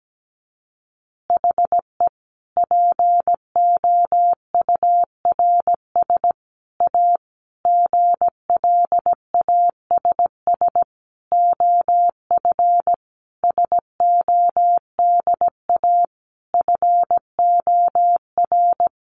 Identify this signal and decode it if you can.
Morse code, heard as a keyed tone, about 17 wpm, 700 Hz. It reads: HE POURS A GLASS OF SODA FOR
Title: Additional Easy Phrases 17wpm